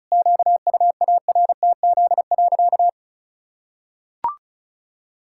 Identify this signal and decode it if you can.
Morse code, heard as a keyed tone, about 36 wpm, 700 Hz. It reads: QUARTZ.
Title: Pangram Challenge v2 - 35wpm